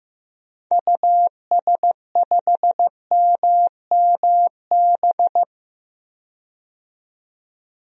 Morse code, heard as a keyed tone, about 15 wpm, 700 Hz. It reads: US5MMB
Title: DXCC-100 Call Signs - 6 Characters - 15wpm